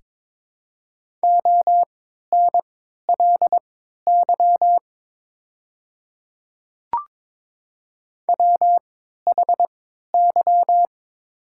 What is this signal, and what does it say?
Morse code, heard as a keyed tone, about 22 wpm, spaced as for 12 wpm, 700 Hz. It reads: ONLY WHY
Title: 3x Character Spacing - Top 200 Words - 22wpm